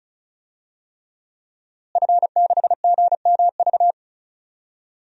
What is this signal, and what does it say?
Morse code, heard as a keyed tone, about 34 wpm, 700 Hz. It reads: F6GMV
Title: DXCC-100 Call Signs - 5 Characters - Repeated 3x - 35wpm